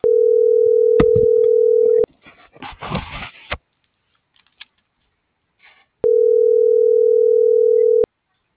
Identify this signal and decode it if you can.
Ringback tone (North America), 440 + 480 Hz.